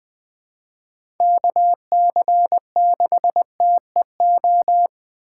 Morse code, heard as a keyed tone, about 20 wpm, 700 Hz. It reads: KC6TEO
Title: USA Call Signs - 6 Characters - 3x Character Spacing - 20wpm